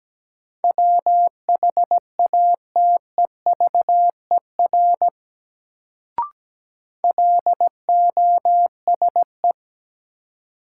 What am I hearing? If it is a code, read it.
Morse code, heard as a keyed tone, about 17 wpm, 700 Hz. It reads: WHATEVER LOSE
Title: Sentences from Top 1100 Words - New Word Drill Alt - 17wpm